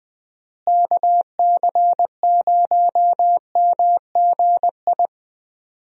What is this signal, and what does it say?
Morse code, heard as a keyed tone, about 20 wpm, 700 Hz. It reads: KC0MGI
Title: USA Call Signs - 6 Characters - Repeated 3x - 20wpm